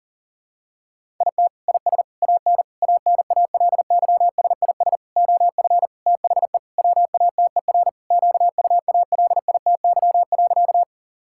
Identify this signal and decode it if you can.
Morse code, heard as a keyed tone, about 40 wpm, 700 Hz. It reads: IT IS AN ANALYSIS OF THE WATER QUALITY.
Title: Sentences from Top 1000 Words - Review Part 5 - 40wpm